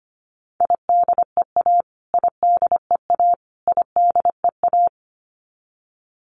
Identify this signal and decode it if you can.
Morse code, heard as a keyed tone, about 25 wpm, 700 Hz. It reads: IDEA IDEA IDEA
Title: Top 500 Words Repeated 25wpm